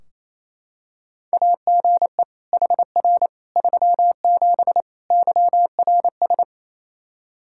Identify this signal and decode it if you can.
Morse code, heard as a keyed tone, about 28 wpm, 700 Hz. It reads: AGE HR 37 YRS
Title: Practice QSO - Exchange 3 - More Info - 28wpm